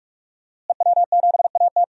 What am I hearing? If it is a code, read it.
Morse code, heard as a keyed tone, about 42 wpm, 700 Hz. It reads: EW7AT
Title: DXCC-100 Call Signs - 5 Characters - Repeated 3x - 45wpm